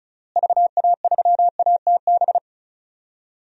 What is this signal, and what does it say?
Morse code, heard as a keyed tone, about 36 wpm, 700 Hz. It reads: VA3ATB
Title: DXCC-100 Call Signs - 6 Characters - Repeated 3x - 35wpm